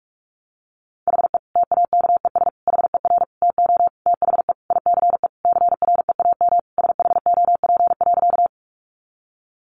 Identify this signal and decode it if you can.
Morse code, heard as a keyed tone, about 44 wpm, 700 Hz. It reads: HE TAKES HER TO THE ICE CREAM SHOP.
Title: Easy Phrases 45wpm